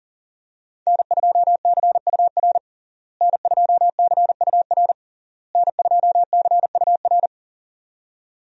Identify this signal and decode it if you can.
Morse code, heard as a keyed tone, about 37 wpm, 700 Hz. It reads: N2CUR N2CUR N2CUR
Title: USA Call Signs - 5 Characters - Repeated 3x - 40wpm